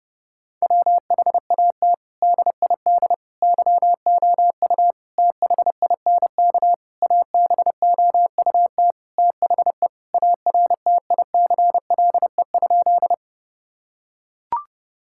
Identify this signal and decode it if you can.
Morse code, heard as a keyed tone, about 30 wpm, 700 Hz. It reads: WHAT DID YOU THINK ABOUT THE ARTICLE?
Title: Sentences from Top 1000 Words - Review Part 5 - 30wpm